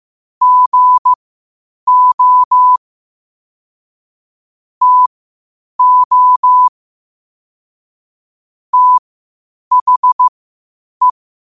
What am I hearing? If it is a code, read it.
Morse code, heard as a keyed tone, about 15 wpm, spaced as for 8 wpm, 1000 Hz. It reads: GO TO THE
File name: qotdfiles / 8wpm